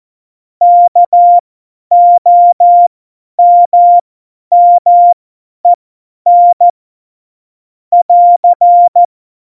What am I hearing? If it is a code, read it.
Morse code, heard as a keyed tone, about 14 wpm, spaced as for 10 wpm, 700 Hz. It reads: KOMMEN <AR>
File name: Morsekurs / Witze / Witz 28 April 2020, 50 F70